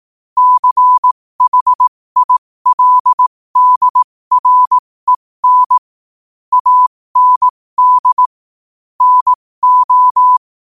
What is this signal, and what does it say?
Morse code, heard as a keyed tone, about 18 wpm, spaced as for 15 wpm, 1000 Hz. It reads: CHILDREN AND NO
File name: qotdfiles / 15wpm